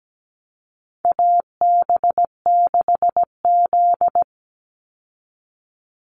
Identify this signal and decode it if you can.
Morse code, heard as a keyed tone, about 17 wpm, 700 Hz. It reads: AB6Z
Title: Call Signs Repeated 3x - 17wpm